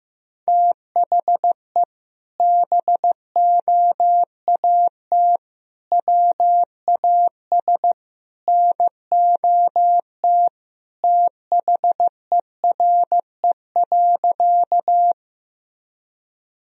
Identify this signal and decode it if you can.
Morse code, heard as a keyed tone, about 15 wpm, 700 Hz. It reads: THE BOAT WAS NOT THERE.
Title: Sentences from Top 1000 Words - Review Part 3 - 15wpm